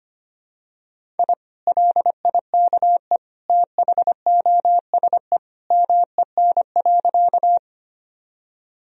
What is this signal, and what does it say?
Morse code, heard as a keyed tone, about 25 wpm, 700 Hz. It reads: I LIKE THOSE MEN.
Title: Sentences from Top 1000 Words - Review Part 1 - 25wpm